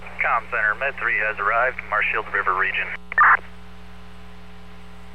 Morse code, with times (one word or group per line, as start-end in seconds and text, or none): none